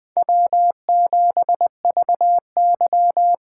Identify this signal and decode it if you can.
Morse code, heard as a keyed tone, about 20 wpm, 700 Hz. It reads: W7VY
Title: USA Call Signs - 4 Characters - 8x Character Spacing - 20wpm